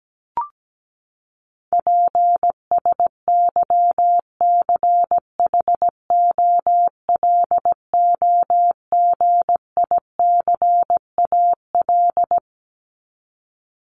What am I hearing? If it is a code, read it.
Morse code, heard as a keyed tone, about 17 wpm, 700 Hz. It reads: PSYCHOLOGICAL
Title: Top 1251-1500 Words 17wpm